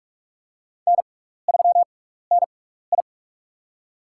Morse code, heard as a keyed tone, about 44 wpm, spaced as for 15 wpm, 700 Hz. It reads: N3NI
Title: USA Call Signs - 6x Character Spacing - 45wpm